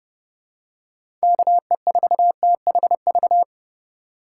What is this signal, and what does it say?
Morse code, heard as a keyed tone, about 30 wpm, 700 Hz. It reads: KE4THV
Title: USA Call Signs - 6 Characters - 2x Character Spacing - 30wpm